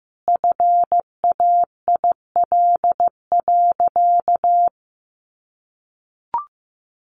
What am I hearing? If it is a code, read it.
Morse code, heard as a keyed tone, about 15 wpm, 700 Hz. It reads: FAIL.
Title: Sentences from Top 1100 Words 15wpm